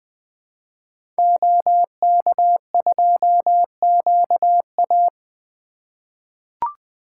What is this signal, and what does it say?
Morse code, heard as a keyed tone, about 20 wpm, 700 Hz. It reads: OK2QA
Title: 2023 SOTA Activator Call Signs - 20wpm